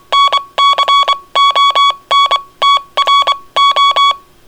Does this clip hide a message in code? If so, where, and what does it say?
Yes: Morse code from 0.1 to 4.1 s, heard as a keyed tone, about 24 wpm, spaced as for 20 wpm, 1120 Hz. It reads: NCONTRO